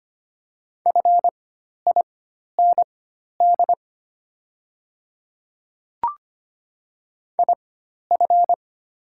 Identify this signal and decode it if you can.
Morse code, heard as a keyed tone, about 25 wpm, spaced as for 12 wpm, 700 Hz. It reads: FIND IF
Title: 4x Character Spacing - Top 100 Words - 25wpm